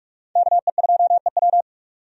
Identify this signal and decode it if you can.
Morse code, heard as a keyed tone, about 46 wpm, 700 Hz. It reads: KE2EW
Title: USA Call Signs - 5 Characters - 45wpm